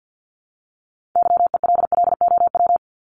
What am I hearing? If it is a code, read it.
Morse code, heard as a keyed tone, about 50 wpm, 700 Hz. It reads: YELLOW
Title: Top 1751-2000 Words 50wpm